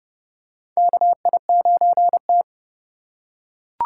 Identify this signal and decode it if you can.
Morse code, heard as a keyed tone, about 30 wpm, 700 Hz. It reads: KI9T
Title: USA Call Signs - 4 Characters - 2x Character Spacing - 30wpm